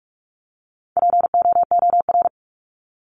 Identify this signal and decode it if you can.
Morse code, heard as a keyed tone, about 44 wpm, 700 Hz. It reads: POOR
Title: Top 751-1000 Words 45wpm